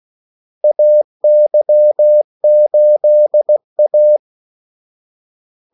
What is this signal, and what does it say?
Morse code, heard as a keyed tone, about 16 wpm, 600 Hz. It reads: AY8A
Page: Kaldesignaler 16 wpm